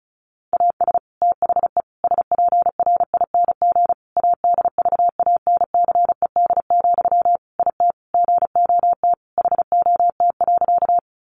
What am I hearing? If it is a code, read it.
Morse code, heard as a keyed tone, about 34 wpm, 700 Hz. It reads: AS THE SPRING ADVANCED, IT GOT HOT.
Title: Sentences from Top 1200 Words 35wpm